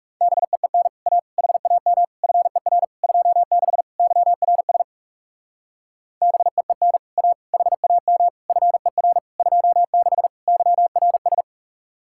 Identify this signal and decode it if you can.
Morse code, heard as a keyed tone, about 37 wpm, 700 Hz. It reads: BEEN A HAM FER 26 YRS BEEN A HAM FER 26 YRS
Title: Speed-Racing Common QSO Phrases Encore - 30wpm